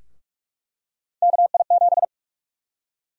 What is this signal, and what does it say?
Morse code, heard as a keyed tone, about 46 wpm, 700 Hz. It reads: KI7
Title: USA Call Sign Prefixes - Repeated 2x - 45wpm